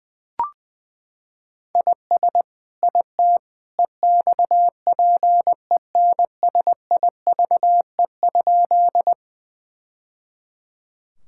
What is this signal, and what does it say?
Morse code, heard as a keyed tone, about 20 wpm, 700 Hz. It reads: IS IT EXPENSIVE?
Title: Sentences from Top 1000 Words - Review Part 6 - 20wpm